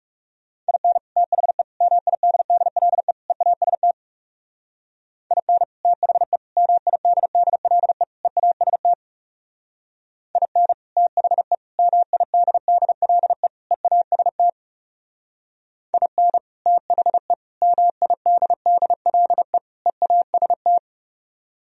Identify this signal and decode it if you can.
Morse code, heard as a keyed tone, about 35 wpm, 700 Hz. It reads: IN THE MIDDLE EAST IN THE MIDDLE EAST IN THE MIDDLE EAST IN THE MIDDLE EAST
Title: Speed-Racing - Sets of 4 Words 30wpm